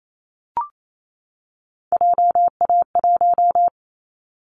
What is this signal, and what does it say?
Morse code, heard as a keyed tone, about 28 wpm, 700 Hz. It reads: JA1
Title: Callsign Prefixes for Top 50 DXCC - 28wpm